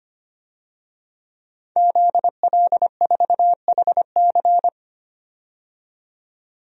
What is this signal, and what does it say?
Morse code, heard as a keyed tone, about 25 wpm, 700 Hz. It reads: ZL4HC
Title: DXCC-100 Call Signs - 5 Characters - 25wpm